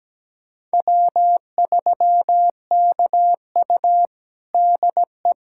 Morse code, heard as a keyed tone, about 17 wpm, 700 Hz. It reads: W3KU DE
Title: USA Callsign DE Callsign - 4 Characters - 17wpm